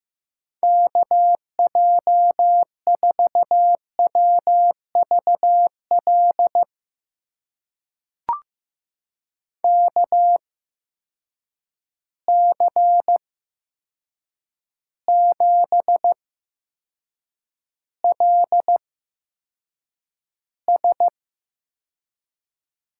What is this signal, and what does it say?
Morse code, heard as a keyed tone, about 16 wpm, spaced as for 7 wpm, 700 Hz. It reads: KJ4WVL KC7LS
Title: USA Call Signs - 6 Characters - 8x Character Spacing - 15wpm